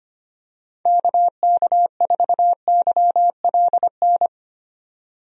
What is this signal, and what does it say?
Morse code, heard as a keyed tone, about 25 wpm, 700 Hz. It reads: KK4YLN
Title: USA Call Signs - 6 Characters - 25wpm